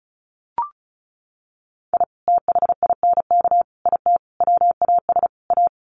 Morse code, heard as a keyed tone, about 34 wpm, 700 Hz. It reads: I THINK IT WAS A
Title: Sets of 5 Words 35wpm